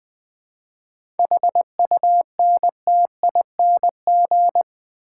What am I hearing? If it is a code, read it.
Morse code, heard as a keyed tone, about 20 wpm, 700 Hz. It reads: HUNTING
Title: 7 Letter Words - First 3 Letters Given - 20wpm